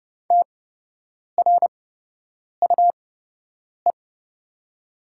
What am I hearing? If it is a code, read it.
Morse code, heard as a keyed tone, about 30 wpm, spaced as for 8 wpm, 700 Hz. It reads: TRUE
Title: 8x Character Spacing - Top 500 Words - 30wpm